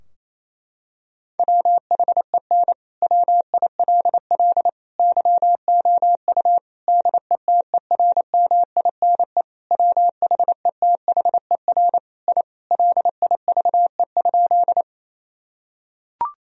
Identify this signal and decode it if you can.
Morse code, heard as a keyed tone, about 28 wpm, 700 Hz. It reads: WHEN WILL YOU DETERMINE WHETHER I LIVE?
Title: Sentences from Top 1000 Words - Review Part 4 - 28wpm